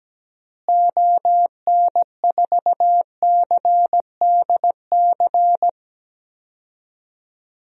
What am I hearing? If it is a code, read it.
Morse code, heard as a keyed tone, about 17 wpm, 700 Hz. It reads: ON4CDC
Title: DXCC-100 Call Signs - 6 Characters - Repeated 3x - 17wpm